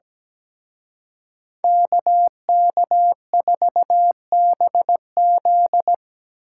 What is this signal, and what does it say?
Morse code, heard as a keyed tone, about 19 wpm, 700 Hz. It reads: KK4BZ
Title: Speed Racing - USA 2x2 Call Signs 17wpm